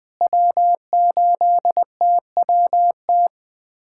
Morse code, heard as a keyed tone, about 21 wpm, 700 Hz. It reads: W8TWT
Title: USA Callsign DE Callsign - 5 Characters - 20wpm